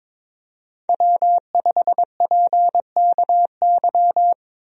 Morse code, heard as a keyed tone, about 22 wpm, 700 Hz. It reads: W5PKY
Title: USA Call Signs - 5 Characters - Repeated 3x - 22wpm